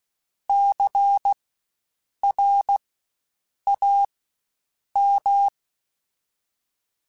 Morse code, heard as a keyed tone, about 16 wpm, spaced as for 7 wpm, 800 Hz. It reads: CRAM
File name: Spaced Alphas / Spaced Alphas mp3s / Cs sp @15